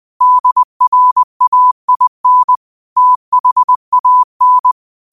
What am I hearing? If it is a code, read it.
Morse code, heard as a keyed tone, about 20 wpm, 1000 Hz. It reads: DRAIN THAN